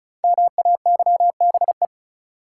Morse code, heard as a keyed tone, about 36 wpm, 700 Hz. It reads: MAYBE
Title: Sentences from Top 1000 Words - New Word Drill - 35wpm